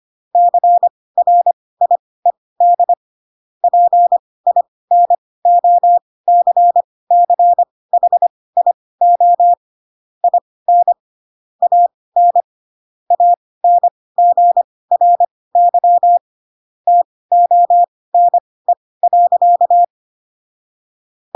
Morse code, heard as a keyed tone, about 25 wpm, spaced as for 18 wpm, 700 Hz. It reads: CRIED PINOCCHIO IN AN ANGRY TONE.
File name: Pinocchio / Pinocchio ch7 12 25 ik1hge